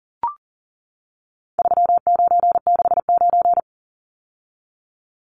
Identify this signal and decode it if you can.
Morse code, heard as a keyed tone, about 40 wpm, 700 Hz. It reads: 3969
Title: Numbers - 4-Digit 40wpm